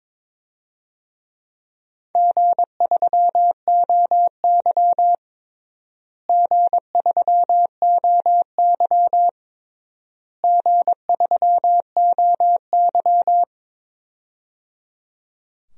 Morse code, heard as a keyed tone, about 21 wpm, 700 Hz. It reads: G3OY G3OY G3OY
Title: DXCC-100 Call Signs - 4 Characters - Repeated 3x - 22wpm